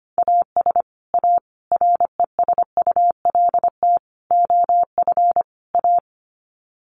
Morse code, heard as a keyed tone, about 25 wpm, 700 Hz. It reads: AS A RESULT OF A
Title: Sets of 5 Words Encore 25wpm